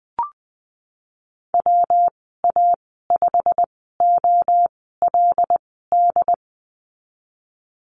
Morse code, heard as a keyed tone, about 19 wpm, spaced as for 14 wpm, 700 Hz. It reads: WA5OLD
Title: 147 Lesson 24g - Letter y - Call Signs 20wpm